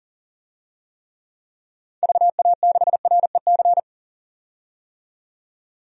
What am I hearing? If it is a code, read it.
Morse code, heard as a keyed tone, about 40 wpm, 700 Hz. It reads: VA6REC
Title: DXCC-100 Call Signs - 6 Characters - 40wpm